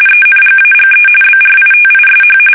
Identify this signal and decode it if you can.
RTTY, about 45.45 baud. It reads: ENGAGE BRITISH
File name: fsk850 / usnavy1